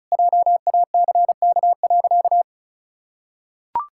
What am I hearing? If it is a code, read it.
Morse code, heard as a keyed tone, about 34 wpm, 700 Hz. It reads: JACK.
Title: Pangram Challenge v2 - 35wpm